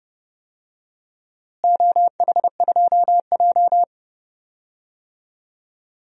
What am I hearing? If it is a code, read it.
Morse code, heard as a keyed tone, about 30 wpm, 700 Hz. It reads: OH2J
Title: DXCC-100 Call Signs - 4 Characters - Repeated 2x - 30wpm